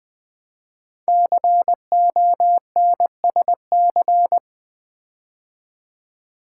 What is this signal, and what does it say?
Morse code, heard as a keyed tone, about 20 wpm, 700 Hz. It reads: CONSC
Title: Build A Word - 5 Letters At A Time - 10-Letter Words - 20wpm